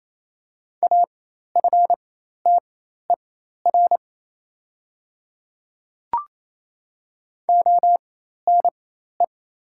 Morse code, heard as a keyed tone, about 27 wpm, spaced as for 13 wpm, 690 Hz. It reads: AFTER ONE
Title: 4x Character Spacing - Top 200 Words - 28wpm